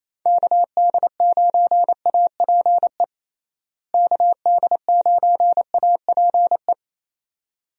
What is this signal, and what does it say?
Morse code, heard as a keyed tone, about 28 wpm, 700 Hz. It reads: KD9APE KD9APE
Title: USA Call Signs - 6 Characters - Repeated 2x - 28wpm